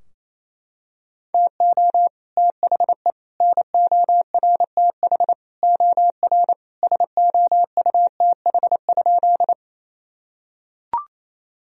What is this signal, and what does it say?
Morse code, heard as a keyed tone, about 28 wpm, 700 Hz. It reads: TO THE NORTH OR SOUTH?
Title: Sentences from Top 1000 Words - Review Part 2 - 28wpm